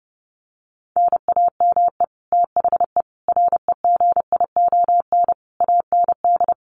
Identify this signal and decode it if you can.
Morse code, heard as a keyed tone, about 30 wpm, 700 Hz. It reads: NAME THE REGION AND
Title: Sentences from Top 1500 Words 30wpm